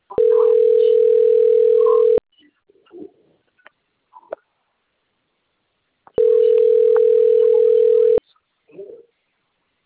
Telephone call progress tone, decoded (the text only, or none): ringback tone